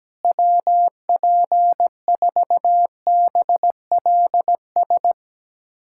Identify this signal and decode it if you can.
Morse code, heard as a keyed tone, about 17 wpm, 700 Hz. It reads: WP4BLS
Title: USA Callsign DE Callsign - 6 Characters - 17wpm